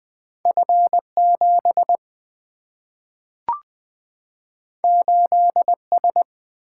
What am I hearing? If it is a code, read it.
Morse code, heard as a keyed tone, about 20 wpm, 700 Hz. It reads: F 7 8 S
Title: Single-Letter Single-Number 20wpm